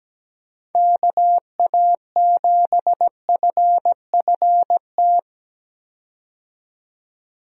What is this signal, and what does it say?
Morse code, heard as a keyed tone, about 17 wpm, 700 Hz. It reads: KA7FFT
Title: USA Call Signs - 6 Characters - 17wpm